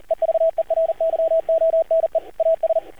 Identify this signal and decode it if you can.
Morse code, heard as a keyed tone, about 38 wpm, 650 Hz, with radio noise. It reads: EVERYONE AS
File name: archived / CRSnet